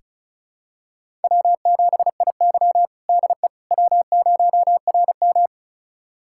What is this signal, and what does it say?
Morse code, heard as a keyed tone, about 36 wpm, 700 Hz. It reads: W7IY DE W0RM
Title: USA Callsign DE Callsign - 4 Characters - 35wpm